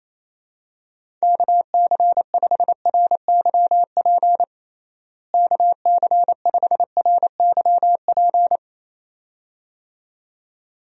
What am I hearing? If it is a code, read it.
Morse code, heard as a keyed tone, about 28 wpm, 700 Hz. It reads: KC5RYP KC5RYP
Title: USA Call Signs - 6 Characters - Repeated 2x - 28wpm